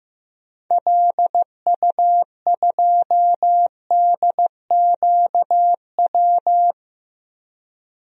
Morse code, heard as a keyed tone, about 15 wpm, 700 Hz. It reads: LU2DQW
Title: DXCC-100 Call Signs - 6 Characters - Repeated 3x - 15wpm